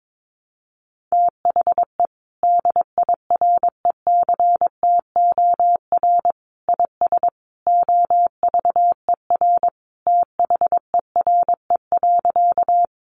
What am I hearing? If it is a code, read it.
Morse code, heard as a keyed tone, about 22 wpm, 700 Hz. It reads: THE DIRECTOR IS OVER THERE.
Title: Sentences from Top 1300 Words 22wpm